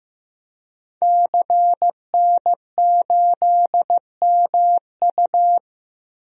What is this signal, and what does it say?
Morse code, heard as a keyed tone, about 15 wpm, 700 Hz. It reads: CN8MU
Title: DXCC-100 Call Signs - 5 Characters - Repeated 3x - 15wpm